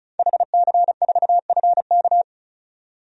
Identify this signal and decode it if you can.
Morse code, heard as a keyed tone, about 34 wpm, 700 Hz. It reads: HC4FK
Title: DXCC-100 Call Signs - 5 Characters - 35wpm